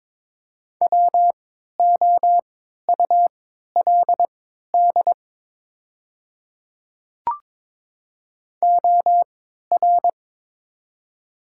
Morse code, heard as a keyed tone, about 23 wpm, spaced as for 13 wpm, 700 Hz. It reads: WOULD OR
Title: 3x Character Spacing - Top 100 Words - 22wpm